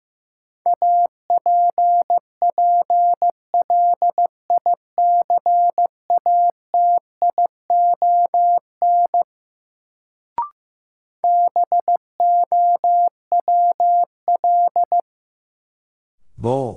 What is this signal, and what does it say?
Morse code, heard as a keyed tone, about 15 wpm, 700 Hz. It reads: APPLICATION BOWL
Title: Sentences from Top 1200 Words - New Word Drill - 15wpm